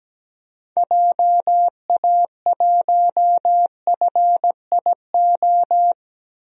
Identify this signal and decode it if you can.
Morse code, heard as a keyed tone, about 17 wpm, 700 Hz. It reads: JA1FIO
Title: DXCC-100 Call Signs - 6 Characters - 17wpm